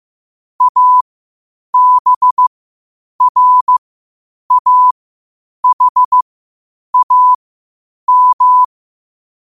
Morse code, heard as a keyed tone, about 15 wpm, spaced as for 8 wpm, 1000 Hz. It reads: ABRAHAM
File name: qotdfiles / 8wpm